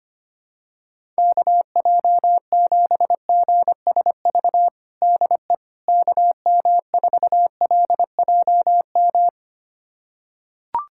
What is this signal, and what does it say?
Morse code, heard as a keyed tone, about 25 wpm, 700 Hz. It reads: KJ7GSV DE KM4LJM
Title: USA Callsign DE Callsign - 6 Characters - 25wpm